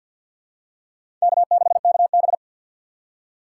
Morse code, heard as a keyed tone, about 52 wpm, 700 Hz. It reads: K6KB
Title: USA Call Signs - 2x Character Spacing - 50wpm